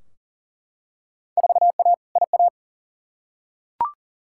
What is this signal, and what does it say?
Morse code, heard as a keyed tone, about 40 wpm, 700 Hz. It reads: 4A IA
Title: Field Day Exchanges - 40wpm